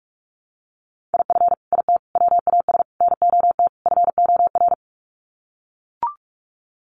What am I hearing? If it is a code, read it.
Morse code, heard as a keyed tone, about 44 wpm, 700 Hz. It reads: IF IT WAS NOT FOR
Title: Sets of 5 Words Encore 45wpm